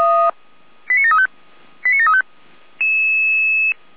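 Five-tone selective call (ZVEI), heard as quick runs of five tones, tones of about 80 ms. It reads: 89735 89735
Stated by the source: ZVEI five-tone selective call received by SDR.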